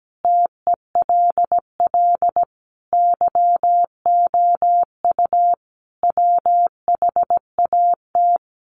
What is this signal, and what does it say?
Morse code, heard as a keyed tone, about 17 wpm, 700 Hz. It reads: TELL YOU WHAT
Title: A Sound Of Thunder 17wpm